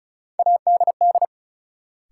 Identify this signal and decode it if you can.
Morse code, heard as a keyed tone, about 36 wpm, 700 Hz. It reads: ADD
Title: Sentences from Top 1000 Words - Review Part 1 - New Word Drill - 35wpm